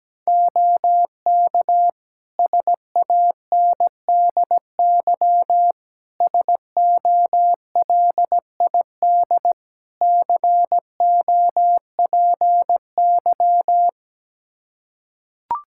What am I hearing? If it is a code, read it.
Morse code, heard as a keyed tone, about 17 wpm, 700 Hz. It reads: OK SANDY SOLID COPY
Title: K1USN SST QSO Practice 17wpm